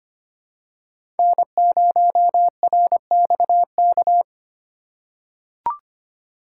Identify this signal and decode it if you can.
Morse code, heard as a keyed tone, about 25 wpm, 700 Hz. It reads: N0RXK
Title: USA Call Signs - 5 Characters - 3x Character Spacing - 25wpm